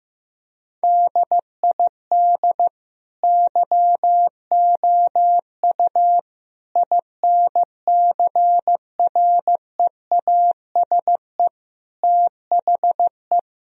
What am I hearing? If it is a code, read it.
Morse code, heard as a keyed tone, about 15 wpm, 700 Hz. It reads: DID YOU INCREASE THE
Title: Sentences from Top 1000 Words - Review Part 4 - 15wpm